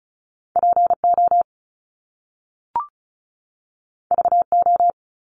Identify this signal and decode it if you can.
Morse code, heard as a keyed tone, about 35 wpm, 700 Hz. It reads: PO VO
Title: Top Two Letter Combinations in Words 35wpm